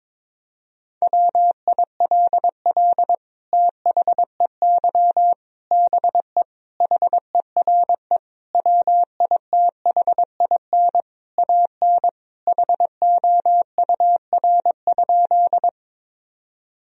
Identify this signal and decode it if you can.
Morse code, heard as a keyed tone, about 22 wpm, 700 Hz. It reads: WILL THEY BE HERE WITHIN AN HOUR?
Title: Sentences from Top 1000 Words - Review Part 4 - 22wpm